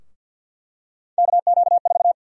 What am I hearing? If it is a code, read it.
Morse code, heard as a keyed tone, about 47 wpm, 700 Hz. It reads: KQ4
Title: USA Call Sign Prefixes - 50wpm